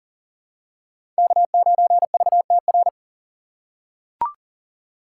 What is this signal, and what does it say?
Morse code, heard as a keyed tone, about 37 wpm, 700 Hz. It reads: K9VTR
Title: USA Call Signs - 5 Characters - 2x Character Spacing - 40wpm